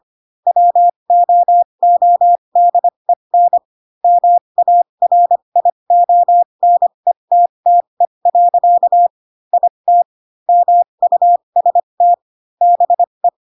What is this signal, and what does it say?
Morse code, heard as a keyed tone, about 25 wpm, spaced as for 22 wpm, 700 Hz. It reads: WOODEN MARIONETTE. IT MUST BE
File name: Pinocchio / Pinocchio ch2 18 25 ik1hge